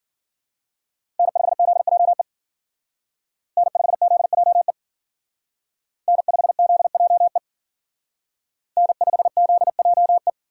Morse code, heard as a keyed tone, about 51 wpm, 700 Hz. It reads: N5ZJE N5ZJE N5ZJE N5ZJE